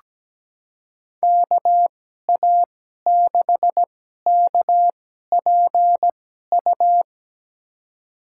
Morse code, heard as a keyed tone, about 17 wpm, spaced as for 12 wpm, 700 Hz. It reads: KA6KPU
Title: USA Call Signs - 6 Characters - 2x Character Spacing - 17wpm